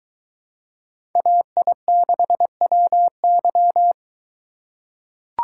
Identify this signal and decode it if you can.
Morse code, heard as a keyed tone, about 23 wpm, 700 Hz. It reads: AI6WY